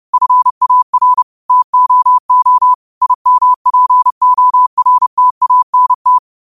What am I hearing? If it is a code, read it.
Morse code, heard as a keyed tone, about 30 wpm, 1000 Hz. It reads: FAR TOO IMPORTANT